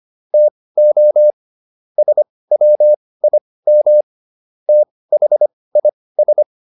Morse code, heard as a keyed tone, about 25 wpm, spaced as for 18 wpm, 600 Hz. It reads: TO SWIM THIS